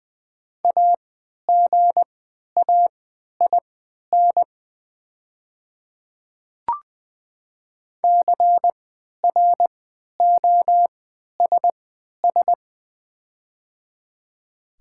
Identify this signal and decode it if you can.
Morse code, heard as a keyed tone, about 20 wpm, spaced as for 11 wpm, 700 Hz. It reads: AGAIN CROSS
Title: 3x Character Spacing - Top 300 Words - 20wpm